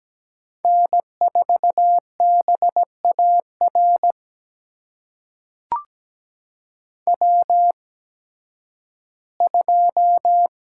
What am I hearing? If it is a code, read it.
Morse code, heard as a keyed tone, about 17 wpm, spaced as for 11 wpm, 700 Hz. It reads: N4BAR W2
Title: USA Call Signs - 5 Characters - 8x Character Spacing - 17wpm